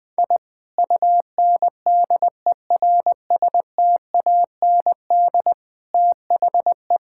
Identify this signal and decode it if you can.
Morse code, heard as a keyed tone, about 21 wpm, 700 Hz. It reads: I UNDERSTAND THE
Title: Sentences from Top 1000 Words - Review Part 6 - 20wpm